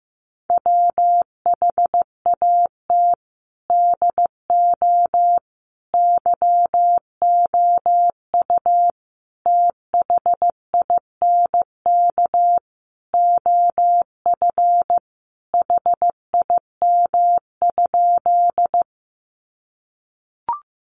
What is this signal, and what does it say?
Morse code, heard as a keyed tone, about 15 wpm, 700 Hz. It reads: WHAT DO YOU THINK OF HIM?
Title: Sentences from Top 200 Words 15wpm